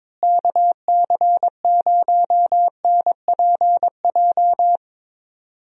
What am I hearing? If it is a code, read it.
Morse code, heard as a keyed tone, about 22 wpm, 700 Hz. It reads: KC0NPJ
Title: USA Call Signs - 6 Characters - Repeated 3x - 22wpm